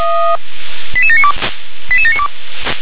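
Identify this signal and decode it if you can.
Five-tone selective call (ZVEI), heard as quick runs of five tones, tones of about 70 ms. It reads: 88792 88792
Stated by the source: ZVEI five-tone call received by SDR.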